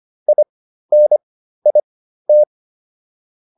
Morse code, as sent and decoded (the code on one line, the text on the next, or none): .. -. .. -
INIT